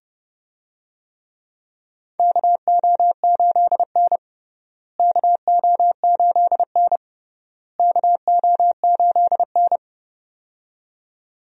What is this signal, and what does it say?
Morse code, heard as a keyed tone, about 30 wpm, 700 Hz. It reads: KO8N KO8N KO8N
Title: USA Call Signs - 4 Characters - Repeated 3x - 30wpm